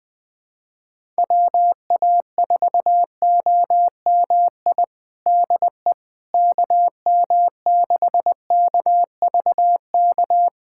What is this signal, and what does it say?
Morse code, heard as a keyed tone, about 20 wpm, 700 Hz. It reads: WA4OMI DE KM6KVK
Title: USA Callsign DE Callsign - 6 Characters - 20wpm